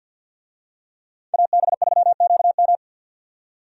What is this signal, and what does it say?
Morse code, heard as a keyed tone, about 47 wpm, 700 Hz. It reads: AB3QM
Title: USA Call Signs - 5 Characters - 50wpm